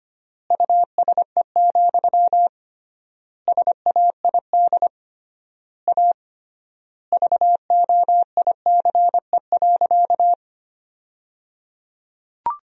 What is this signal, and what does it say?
Morse code, heard as a keyed tone, about 25 wpm, 700 Hz. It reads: USE, SAID A VOICE.
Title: 3x Word-Spacing - Sentences from Top 500 Words - 25wpm